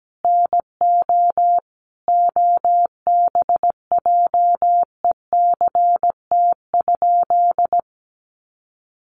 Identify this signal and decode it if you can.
Morse code, heard as a keyed tone, about 17 wpm, 700 Hz. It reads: NO OBJECT?
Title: Sentences from Top 500 Words 17wpm